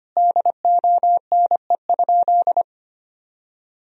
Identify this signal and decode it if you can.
Morse code, heard as a keyed tone, about 25 wpm, 700 Hz. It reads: DONE?
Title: Sentences from Top 1000 Words - Review Part 4 - 25wpm